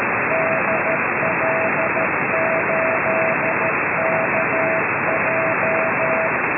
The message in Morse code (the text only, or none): DL8KJ